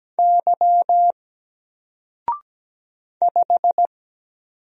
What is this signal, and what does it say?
Morse code, heard as a keyed tone, about 17 wpm, 700 Hz. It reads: Y 5
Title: Single Letter-Number Alt 17wpm